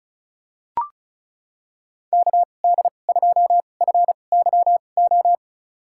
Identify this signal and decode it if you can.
Morse code, heard as a keyed tone, about 34 wpm, spaced as for 25 wpm, 700 Hz. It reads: KD2FYO
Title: USA Call Signs - 6 Characters - 2x Character Spacing - 35wpm